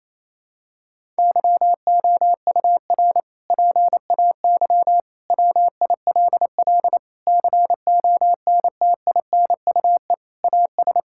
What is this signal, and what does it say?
Morse code, heard as a keyed tone, about 28 wpm, 700 Hz. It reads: YOUR PAY WILL CONTINUE AS
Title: Sentences from Top 1000 Words - Review Part 6 - 28wpm